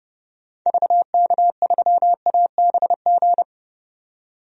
Morse code, heard as a keyed tone, about 31 wpm, 700 Hz. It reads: VK3ABG
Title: DXCC-100 Call Signs - 6 Characters - 30wpm